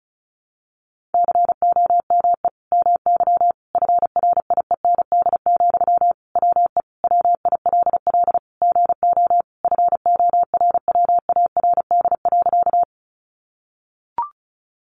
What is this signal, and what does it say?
Morse code, heard as a keyed tone, about 34 wpm, 700 Hz. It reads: COME MY FRIEND, WE WILL GO FORWARD.
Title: Sentences from Top 1300 Words 35wpm